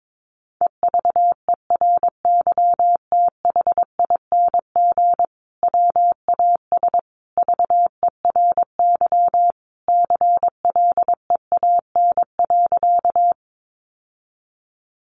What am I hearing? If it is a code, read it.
Morse code, heard as a keyed tone, about 22 wpm, 700 Hz. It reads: EVERYTHING WAS VERY CLEAN.
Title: Sentences from Top 1100 Words 22wpm